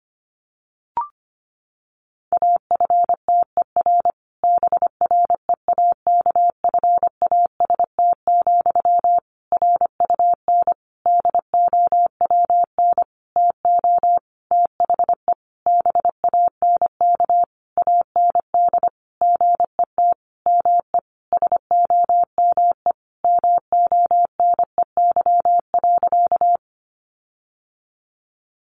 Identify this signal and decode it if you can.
Morse code, heard as a keyed tone, about 25 wpm, 700 Hz. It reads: AFTER BREAKFAST, RUN DOWN TO THE BANK AND GET ME SOME MONEY.
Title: Sentences from Top 1500 Words 25wpm